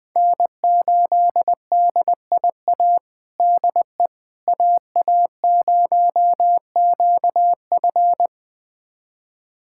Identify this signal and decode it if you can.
Morse code, heard as a keyed tone, about 20 wpm, 700 Hz. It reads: N8DIA DE AA0QF
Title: USA Callsign DE Callsign - 5 Characters - 20wpm